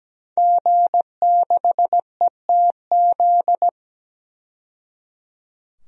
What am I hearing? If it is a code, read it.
Morse code, heard as a keyed tone, about 17 wpm, 700 Hz. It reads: G6ETZ
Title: DXCC-100 Call Signs - 5 Characters - 17wpm